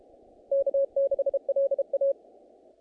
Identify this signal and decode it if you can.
Morse code, heard as a keyed tone, about 32 wpm, 550 Hz. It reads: K6LA
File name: contests / naqp-cw-k5tr-op-k5pi- -1800